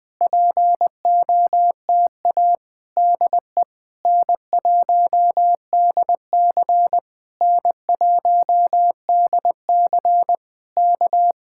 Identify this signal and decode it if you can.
Morse code, heard as a keyed tone, about 20 wpm, 700 Hz. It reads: POTA DE N1DC N1DC K
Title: Speed-Racing Common QSO Phrases - 20wpm